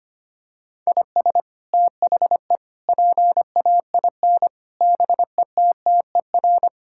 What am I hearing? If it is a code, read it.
Morse code, heard as a keyed tone, about 25 wpm, 700 Hz. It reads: IS THE PAIN BETTER
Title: Sentences from Top 1600 Words 25wpm